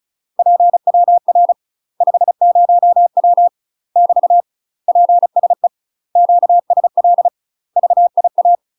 Morse code, heard as a keyed tone, about 34 wpm, 700 Hz. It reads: PWR 50W <BT> PSE QSL VIA
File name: UK MASTER QSOs TRAINING / 35WPM / UK MASTER QSOs 0004